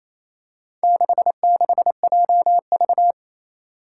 Morse code, heard as a keyed tone, about 28 wpm, 700 Hz. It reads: 66JV
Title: Four Challenging Letter-Number Combinations 28wpm